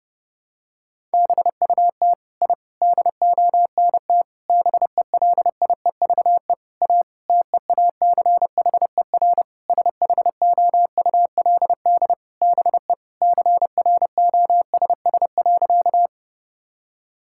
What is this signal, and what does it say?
Morse code, heard as a keyed tone, about 30 wpm, 700 Hz. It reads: BUT I DONT BELIEVE A TEACHER SHOULD BE CROSS.
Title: Sentences from Top 1000 Words - Review Part 6 - 30wpm